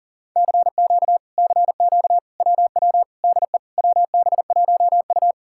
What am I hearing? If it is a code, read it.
Morse code, heard as a keyed tone, about 52 wpm, 700 Hz. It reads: CQ CQ WW DE WB1U